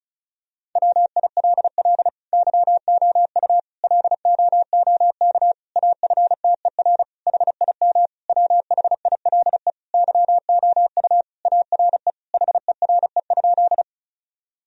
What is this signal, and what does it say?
Morse code, heard as a keyed tone, about 36 wpm, 690 Hz. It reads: WILL YOU LOOK AFTER HIM WHILE YOU ARE HERE?
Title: Sentences from Top 1000 Words - Review Part 1 - 35wpm